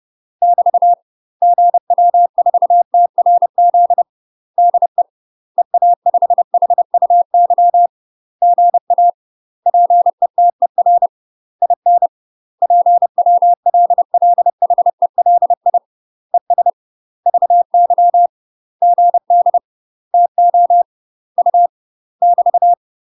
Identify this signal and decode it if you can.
Morse code, heard as a keyed tone, about 30 wpm, 700 Hz. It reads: <BT> GW4TRZ DE EA5HUY GA PETER IN PWLLHELI ES VY GD TO U <BT>
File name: UK MASTER QSOs TRAINING / 30WPM / UK MASTER QSOs 0030